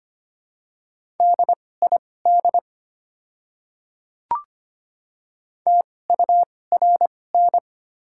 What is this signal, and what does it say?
Morse code, heard as a keyed tone, about 25 wpm, spaced as for 18 wpm, 690 Hz. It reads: DID TURN
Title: 2x Character Spacing - Top 200 Words - 25wpm